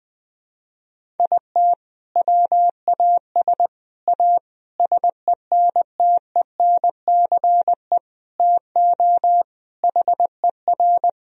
Morse code, heard as a keyed tone, about 20 wpm, 700 Hz. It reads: IT WAS A SENTENCE TO HER
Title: Sentences from Top 1000 Words - Review Part 1 - 20wpm